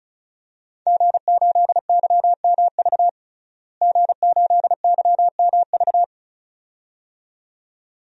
Morse code, heard as a keyed tone, about 36 wpm, 700 Hz. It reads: G8YMV G8YMV
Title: DXCC-100 Call Signs - 5 Characters - Repeated 2x - 35wpm